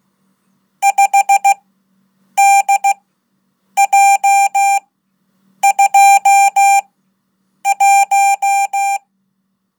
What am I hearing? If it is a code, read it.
Morse code, heard as a keyed tone, about 16 wpm, spaced as for 8 wpm, 790 Hz. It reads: 5DJ21
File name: Code / 5min-1